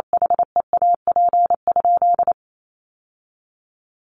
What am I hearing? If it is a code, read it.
Morse code, heard as a keyed tone, about 28 wpm, 700 Hz. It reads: HEAP?
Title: Sentences from Top 1200 Words 28wpm